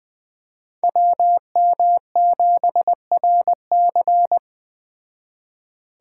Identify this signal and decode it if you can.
Morse code, heard as a keyed tone, about 20 wpm, 700 Hz. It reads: WM7RC